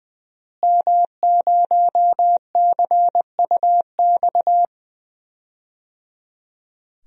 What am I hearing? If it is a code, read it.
Morse code, heard as a keyed tone, about 20 wpm, 700 Hz. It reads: M0CUX
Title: DXCC-100 Call Signs - 5 Characters - Repeated 3x - 20wpm